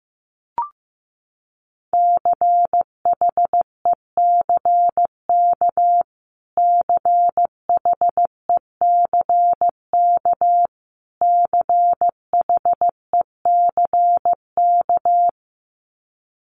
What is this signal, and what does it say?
Morse code, heard as a keyed tone, about 15 wpm, 700 Hz. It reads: CHECK CHECK CHECK
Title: Top 500 Words Repeated 15wpm